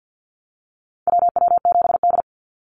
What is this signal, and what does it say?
Morse code, heard as a keyed tone, about 50 wpm, 700 Hz. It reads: WW7D
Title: NA SOTA Activator Callsigns 50wpm